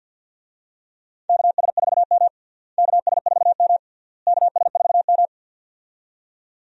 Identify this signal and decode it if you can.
Morse code, heard as a keyed tone, about 52 wpm, 700 Hz. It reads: KS4M KS4M KS4M
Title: USA Call Signs - 4 Characters - Repeated 3x - 50wpm